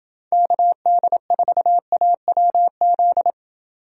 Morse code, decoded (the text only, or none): KD4AWZ